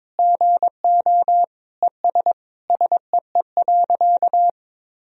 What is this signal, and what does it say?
Morse code, heard as a keyed tone, about 22 wpm, 700 Hz. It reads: GO ES SEE.
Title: Sentences from Top 100 Words and Abbreviations 22wpm